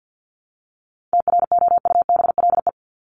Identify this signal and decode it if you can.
Morse code, heard as a keyed tone, about 50 wpm, 700 Hz. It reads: TROUBLE
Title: Top 1001-1250 Words 50wpm